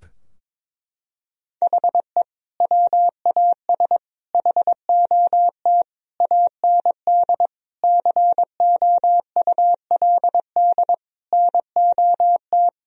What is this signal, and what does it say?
Morse code, heard as a keyed tone, about 22 wpm, 700 Hz. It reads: HE WAS HOT AND COULD NOT
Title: Sentences from Top 1000 Words - Review Part 3 - 22wpm